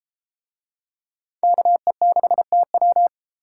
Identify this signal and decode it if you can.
Morse code, heard as a keyed tone, about 32 wpm, 700 Hz. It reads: KE6TW